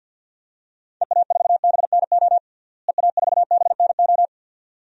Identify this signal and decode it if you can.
Morse code, heard as a keyed tone, about 47 wpm, 700 Hz. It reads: EA4BNO EA4BNO
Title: DXCC-100 Call Signs - 6 Characters - Repeated 2x - 50wpm